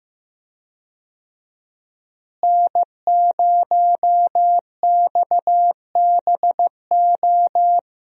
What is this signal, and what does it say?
Morse code, heard as a keyed tone, about 15 wpm, 700 Hz. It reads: N0XBO
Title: USA Call Signs - 5 Characters - Repeated 2x - 15wpm